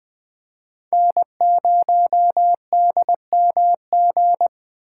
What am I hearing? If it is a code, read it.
Morse code, heard as a keyed tone, about 21 wpm, 700 Hz. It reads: N0DMG
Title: Speed Racing - USA Call Signs 17wpm